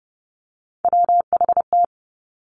Morse code, heard as a keyed tone, about 30 wpm, 700 Hz. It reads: WHT
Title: Three Challenging Letter-Number Combinations 30wpm